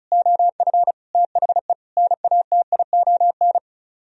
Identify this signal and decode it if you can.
Morse code, heard as a keyed tone, about 41 wpm, 700 Hz. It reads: OF THE NATION
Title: Speed-Racing - Sets of 5 Words 35wpm